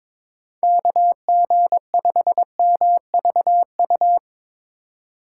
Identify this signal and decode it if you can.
Morse code, heard as a keyed tone, about 22 wpm, 700 Hz. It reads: KG5MVU
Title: USA Call Signs - 4x Character Spacing - 22wpm